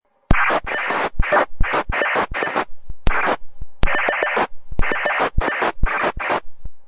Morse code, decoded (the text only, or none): none